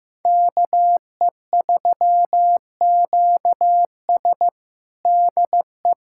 Morse code, heard as a keyed tone, about 15 wpm, 700 Hz. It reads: KE3QS DE
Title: USA Callsign DE Callsign - 5 Characters - 15wpm